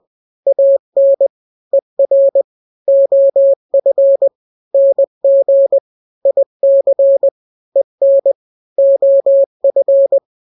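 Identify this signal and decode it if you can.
Morse code, heard as a keyed tone, about 20 wpm, 550 Hz. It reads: AN ER OF NG IC EN OF
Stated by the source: Morse drill of two-letter n-grams at 20/18 WPM (Farnsworth), from KK7RR.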